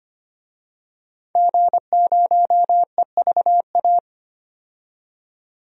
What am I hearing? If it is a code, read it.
Morse code, heard as a keyed tone, about 25 wpm, 700 Hz. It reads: G0EVA
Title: DXCC-100 Call Signs - 5 Characters - 25wpm